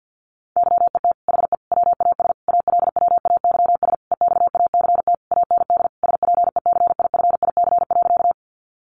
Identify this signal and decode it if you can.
Morse code, heard as a keyed tone, about 50 wpm, 700 Hz. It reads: YET HE WAS ALWAYS EXACT AND SPECIFIC.
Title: Sentences from Top 1300 Words 50wpm